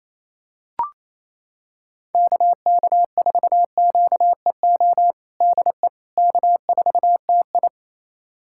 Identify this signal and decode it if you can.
Morse code, heard as a keyed tone, about 28 wpm, 700 Hz. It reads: KK4QEO DE K4TI
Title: USA Callsign DE Callsign 28wpm